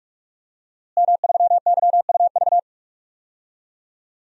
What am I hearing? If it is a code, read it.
Morse code, heard as a keyed tone, about 46 wpm, 700 Hz. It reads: M3YUV
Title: DXCC-100 Call Signs - 5 Characters - Repeated 2x - 45wpm